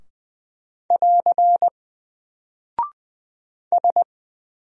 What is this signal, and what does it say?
Morse code, heard as a keyed tone, about 20 wpm, 700 Hz. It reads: <AR> S
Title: Single Character (Letter:Number:Punctuation:Prosign) 20wpm